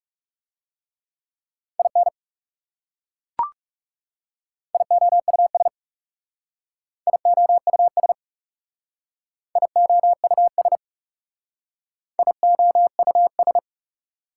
Morse code, heard as a keyed tone, about 41 wpm, 690 Hz. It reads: IN IOUS IOUS IOUS IOUS